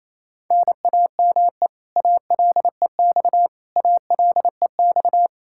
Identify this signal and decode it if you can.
Morse code, heard as a keyed tone, about 28 wpm, 700 Hz. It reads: NAME ALEX ALEX
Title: Practice QSO - Exchange 1 - Beginning - 28wpm